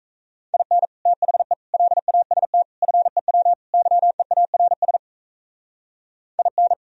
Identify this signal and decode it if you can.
Morse code, heard as a keyed tone, about 36 wpm, 700 Hz. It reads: IN THE LAST FEW YEARS IN